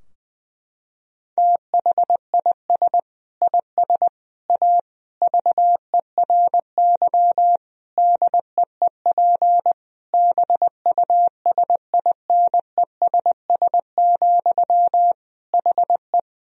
Morse code, heard as a keyed tone, about 21 wpm, 700 Hz. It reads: THIS IS A VERY DEEP BUSINESS, HE
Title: Sentences from Top 1000 Words - Review Part 3 - 20wpm